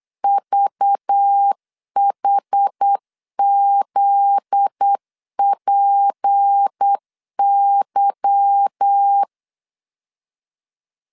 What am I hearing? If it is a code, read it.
Morse code, heard as a keyed tone, about 8 wpm, 800 Hz. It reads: VHZPY
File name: LEZIONI / 8 YJ45 / 8 Y 8wpm 3245